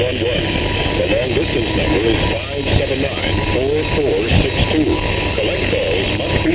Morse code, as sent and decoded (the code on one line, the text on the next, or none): none